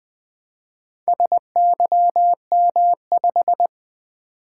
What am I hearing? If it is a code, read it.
Morse code, heard as a keyed tone, about 20 wpm, 700 Hz. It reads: SYM5
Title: Four Letter-Number Combinations - 20wpm